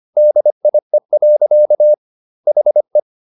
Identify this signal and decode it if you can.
Morse code, heard as a keyed tone, about 25 wpm, 600 Hz. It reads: DIE. HE
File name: the old man and the sea part 1 25wpm